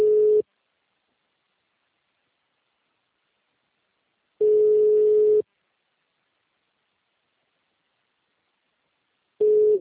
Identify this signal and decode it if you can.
Ringback tone (Europe, 425 Hz).
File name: out- -2041- - - .12699